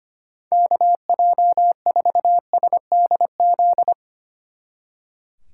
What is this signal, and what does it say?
Morse code, heard as a keyed tone, about 24 wpm, 700 Hz. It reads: KJ4SDZ
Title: Speed Racing - USA 2x3 Call Signs 25wpm